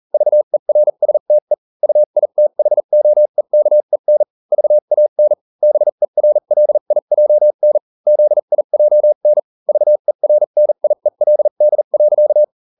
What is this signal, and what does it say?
Morse code, heard as a keyed tone, about 40 wpm, spaced as for 35 wpm, 600 Hz. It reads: VERSTE UITHOEKEN VAN BERLIJN ZIJN VERNIELD.